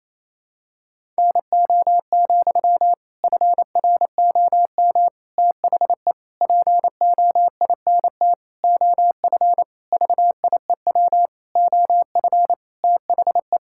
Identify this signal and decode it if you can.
Morse code, heard as a keyed tone, about 28 wpm, 700 Hz. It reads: NO, FROM THE POINT OF VIEW OF THE
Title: Sentences from Top 1000 Words - Review Part 6 - 28wpm